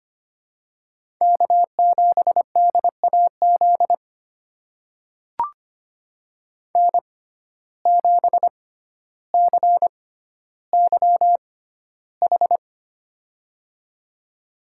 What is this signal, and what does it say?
Morse code, heard as a keyed tone, about 25 wpm, spaced as for 13 wpm, 700 Hz. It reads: K7DAZ N7CYH
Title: USA Call Signs - 5 Characters - 6x Character Spacing - 25wpm